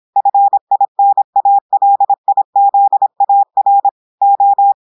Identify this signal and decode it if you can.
Morse code, heard as a keyed tone, about 26 wpm, 800 Hz. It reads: FINALIZAR O
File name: portugais texte en clair 2 vitesse 26 mots minutes